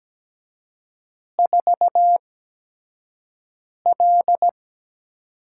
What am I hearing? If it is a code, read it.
Morse code, heard as a keyed tone, about 17 wpm, spaced as for 5 wpm, 700 Hz. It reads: 4L
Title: USA Call Signs - 6 Characters - 8x Character Spacing - 17wpm